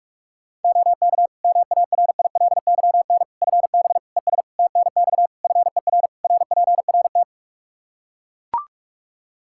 Morse code, heard as a keyed tone, about 42 wpm, 700 Hz. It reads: OK MARILYN FB ES TNX FER RPRT
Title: Practice QSO - Exchange 2 - Info - 45wpm